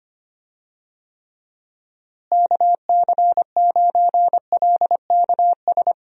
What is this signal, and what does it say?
Morse code, heard as a keyed tone, about 25 wpm, 700 Hz. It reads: KC9LKS
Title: USA Call Signs - 6 Characters - Repeated 3x - 25wpm